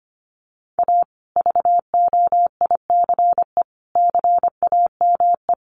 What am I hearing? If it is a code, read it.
Morse code, heard as a keyed tone, about 25 wpm, 700 Hz. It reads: A VOICE CAME
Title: Sentences from Top 1400 Words 25wpm